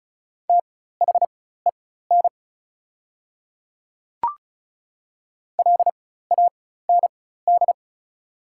Morse code, heard as a keyed tone, about 34 wpm, spaced as for 16 wpm, 690 Hz. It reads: THEN LAND
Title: 4x Character Spacing - Top 300 Words - 35wpm